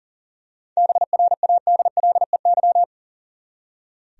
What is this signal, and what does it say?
Morse code, heard as a keyed tone, about 40 wpm, 700 Hz. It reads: BRADLEY
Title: USA Common Names - 40wpm